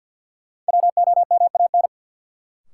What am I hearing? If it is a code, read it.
Morse code, heard as a keyed tone, about 53 wpm, 700 Hz. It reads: WOMAN
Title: Sentences from Top 1100 Words - New Word Drill Alt - 50wpm